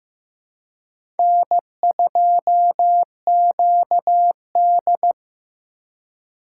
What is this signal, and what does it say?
Morse code, heard as a keyed tone, about 15 wpm, 700 Hz. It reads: N2QD
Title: USA Call Signs - 4 Characters - 4x Character Spacing - 15wpm